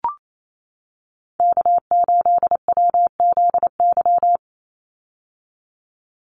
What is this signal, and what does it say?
Morse code, heard as a keyed tone, about 28 wpm, 700 Hz. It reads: K8WZY
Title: USA Call Signs 28wpm